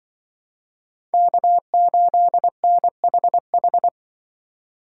Morse code, heard as a keyed tone, about 24 wpm, 700 Hz. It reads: K8NHH
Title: Speed Racing - USA 1x3 Call Signs 20wpm